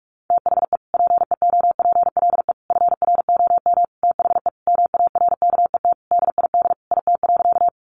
Morse code, heard as a keyed tone, about 44 wpm, 700 Hz. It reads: THE PEOPLE FROM THE MARKET DID IT.
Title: Sentences from Top 600 Words 45wpm